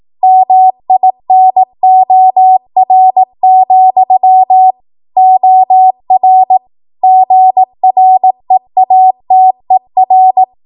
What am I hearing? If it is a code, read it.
Morse code, heard as a keyed tone, about 18 wpm, 750 Hz. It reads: MINOR, OR GREATER
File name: Morse / 18WPM